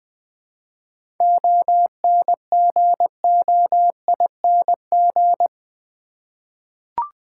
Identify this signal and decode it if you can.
Morse code, heard as a keyed tone, about 20 wpm, 700 Hz. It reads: ONGOING
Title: 7 Letter Words - First 3 Letters Given - 20wpm